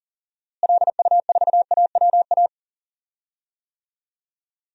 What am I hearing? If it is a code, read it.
Morse code, heard as a keyed tone, about 37 wpm, 700 Hz. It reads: LU4AWA
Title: DXCC-100 Call Signs - 6 Characters - Repeated 3x - 40wpm